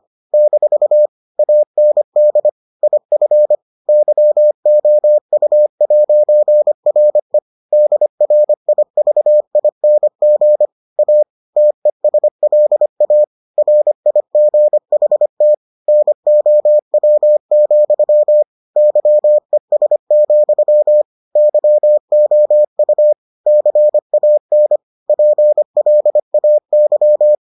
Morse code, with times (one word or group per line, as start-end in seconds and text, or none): 0.3-1.1 -
1.4-2.5 AND
2.8-3.5 IF
3.9-7.4 YOU'RE
7.7-10.7 DRIVING
11.0-11.2 A
11.6-13.2 TESLA
13.6-15.5 RIGHT
15.9-18.4 NOW,
18.8-21.0 YES,
21.4-23.1 YOU
23.5-24.8 CAN
25.1-27.4 PLAY